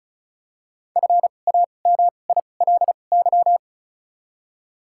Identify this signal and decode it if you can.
Morse code, heard as a keyed tone, about 36 wpm, spaced as for 25 wpm, 700 Hz. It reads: FAMILY
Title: 2x Character Spacing - Top 500 Words - 35wpm